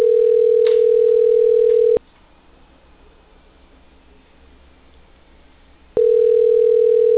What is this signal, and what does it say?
Ringback tone (North America), 440 + 480 Hz.
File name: exten-2041-2051- - - .140